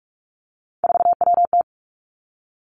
Morse code, heard as a keyed tone, about 46 wpm, 700 Hz. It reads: 4WT
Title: Three Challenging Letter-Number Combinations 45wpm